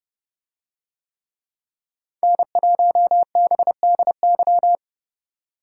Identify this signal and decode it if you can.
Morse code, heard as a keyed tone, about 29 wpm, 700 Hz. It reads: N1BDY